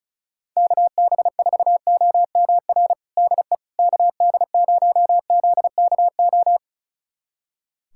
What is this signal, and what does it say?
Morse code, heard as a keyed tone, about 34 wpm, 700 Hz. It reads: KB4OMR DE KD0ZKO
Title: USA Callsign DE Callsign - 6 Characters - 35wpm